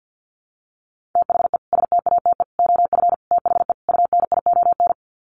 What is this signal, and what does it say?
Morse code, heard as a keyed tone, about 50 wpm, 700 Hz. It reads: THE STATE OF THE UNION
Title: Sets of 5 Words Encore 50wpm